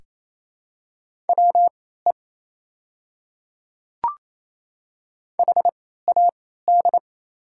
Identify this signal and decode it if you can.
Morse code, heard as a keyed tone, about 27 wpm, spaced as for 16 wpm, 700 Hz. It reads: WE HAD
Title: 3x Character Spacing - Top 300 Words - 28wpm